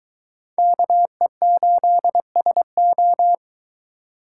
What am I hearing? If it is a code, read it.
Morse code, heard as a keyed tone, about 22 wpm, 700 Hz. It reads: KE8SO